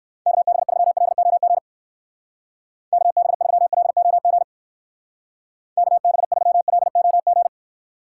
Morse code, heard as a keyed tone, about 58 wpm, 700 Hz. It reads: KB3LOG KB3LOG KB3LOG